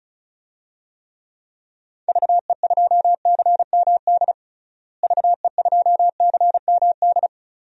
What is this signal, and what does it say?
Morse code, heard as a keyed tone, about 36 wpm, 700 Hz. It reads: VE2CMD VE2CMD
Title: DXCC-100 Call Signs - 6 Characters - Repeated 3x - 35wpm